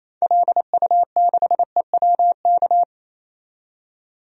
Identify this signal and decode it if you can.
Morse code, heard as a keyed tone, about 28 wpm, 700 Hz. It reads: LU6EWK
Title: DXCC-100 Call Signs - 6 Characters - 28wpm